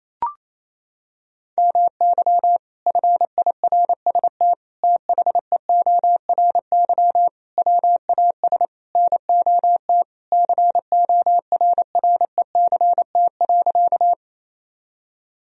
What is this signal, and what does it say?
Morse code, heard as a keyed tone, about 28 wpm, 700 Hz. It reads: MY FIRST THEORY WAS NOT CORRECT.
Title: Sentences from Top 1000 Words - Review Part 3 - 28wpm